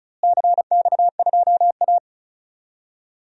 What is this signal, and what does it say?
Morse code, heard as a keyed tone, about 36 wpm, 700 Hz. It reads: CX2A
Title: DXCC-100 Call Signs - 4 Characters - Repeated 3x - 35wpm